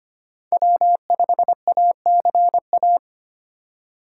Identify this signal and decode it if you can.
Morse code, heard as a keyed tone, about 25 wpm, 700 Hz. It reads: W5ACA
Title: USA Call Signs - 5 Characters - 6x Character Spacing - 25wpm